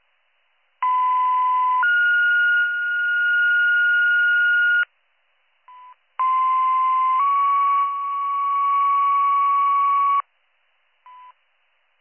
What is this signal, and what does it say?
Two-tone page, heard as a one-second tone, then a three-second tone, about 1010 then 1430 Hz, about 1040 then 1120 Hz. It is heard from 0.8 to 10.2 s.